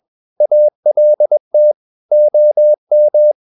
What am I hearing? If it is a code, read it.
Morse code, heard as a keyed tone, about 21 wpm, 600 Hz. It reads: ALT OM
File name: Ferie QRS 21wpm